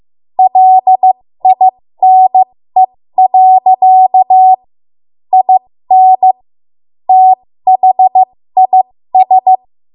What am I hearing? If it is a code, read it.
Morse code, heard as a keyed tone, about 15 wpm, spaced as for 13 wpm, 750 Hz. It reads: LINE. IN THIS
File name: Morse / 13WPM